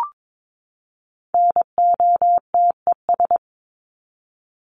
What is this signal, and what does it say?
Morse code, heard as a keyed tone, about 22 wpm, 700 Hz. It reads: NOTES
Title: ES:S Suffix - Top 100 Words with Base Words 22wpm